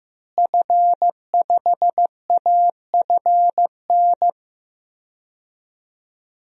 Morse code, heard as a keyed tone, about 15 wpm, 700 Hz. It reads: F5AFN
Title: DXCC-100 Call Signs - 5 Characters - Repeated 3x - 15wpm